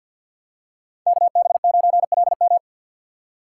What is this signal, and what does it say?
Morse code, heard as a keyed tone, about 50 wpm, 700 Hz. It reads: KB9LM
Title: USA Call Signs - 5 Characters - 4x Character Spacing - 50wpm